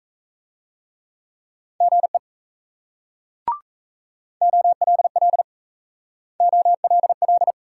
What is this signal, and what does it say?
Morse code, heard as a keyed tone, about 37 wpm, 700 Hz. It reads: GE OLL OLL